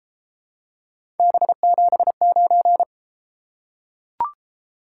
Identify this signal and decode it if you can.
Morse code, heard as a keyed tone, about 32 wpm, 700 Hz. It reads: B79